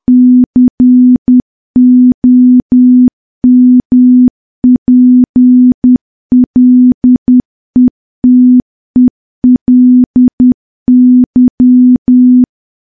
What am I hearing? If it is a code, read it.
Morse code, heard as a keyed tone, about 10 wpm, 250 Hz. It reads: COMPLETELY
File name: morse / morse-training-10WPM